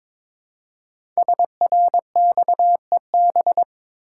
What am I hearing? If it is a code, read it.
Morse code, heard as a keyed tone, about 23 wpm, 700 Hz. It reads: SRXEB
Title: Code Groups - 5 Letters - 22wpm